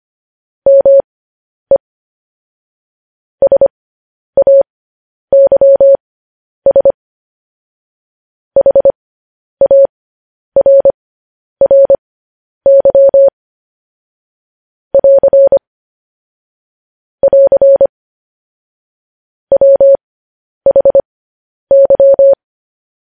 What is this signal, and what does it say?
Morse code, heard as a keyed tone, about 25 wpm, spaced as for 10 wpm, 550 Hz. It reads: ME SAYS HARRY <AR> <AR> WHY
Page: news headlines in morse code at 10 wpm